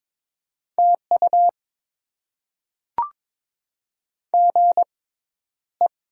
Morse code, heard as a keyed tone, about 22 wpm, spaced as for 11 wpm, 700 Hz. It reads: TU GE
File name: Top 20 QSO Elements - 6x Character Spacing - 22wpm